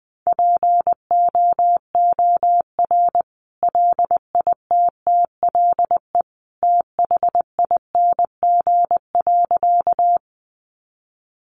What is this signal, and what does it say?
Morse code, heard as a keyed tone, about 20 wpm, 700 Hz. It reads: POOR LITTLE THING.
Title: Sentences from Top 1000 Words 20wpm